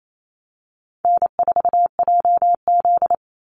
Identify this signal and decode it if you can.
Morse code, heard as a keyed tone, about 28 wpm, 700 Hz. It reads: N4JZ
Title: USA Call Signs 28wpm